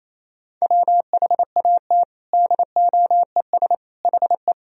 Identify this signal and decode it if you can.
Morse code, heard as a keyed tone, about 28 wpm, 700 Hz. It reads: WHAT DOES HE
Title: Sentences from Top 1000 Words - Review Part 1 - 28wpm